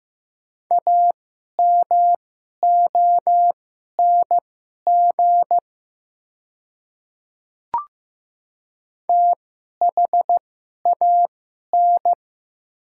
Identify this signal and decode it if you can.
Morse code, heard as a keyed tone, about 15 wpm, spaced as for 11 wpm, 690 Hz. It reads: AMONG THAN
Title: 2x Character Spacing - Top 500 Words - 15wpm